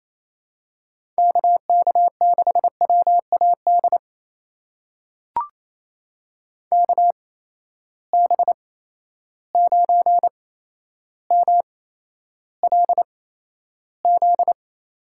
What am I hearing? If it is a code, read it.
Morse code, heard as a keyed tone, about 28 wpm, spaced as for 12 wpm, 700 Hz. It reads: KK6WAD KB9MLZ
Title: USA Call Signs - 8x Character Spacing - 28wpm